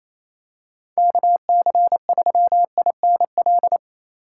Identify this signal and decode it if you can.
Morse code, heard as a keyed tone, about 27 wpm, 700 Hz. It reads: KC3INL
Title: USA Call Signs - 6 Characters - 2x Character Spacing - 28wpm